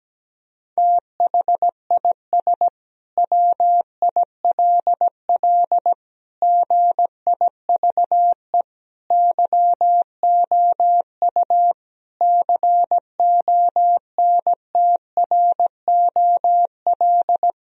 Morse code, heard as a keyed tone, about 17 wpm, 700 Hz. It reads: THIS WILL GIVE YOU CONTROL
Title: Sentences from Top 1000 Words - Review Part 3 - 17wpm